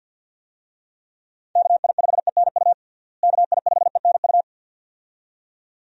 Morse code, heard as a keyed tone, about 47 wpm, 700 Hz. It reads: KI5ENU KI5ENU
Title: USA Call Signs - 6 Characters - Repeated 2x - 50wpm